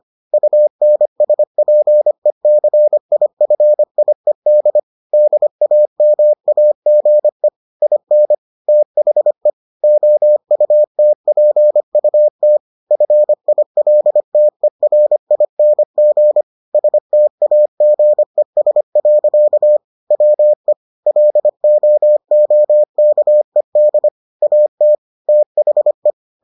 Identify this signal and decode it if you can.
Morse code, heard as a keyed tone, about 25 wpm, 600 Hz. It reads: UNSPECIFIED DAMAGE IN THE OUTPUT FILTERING STAGES. WE LOOKED AT THE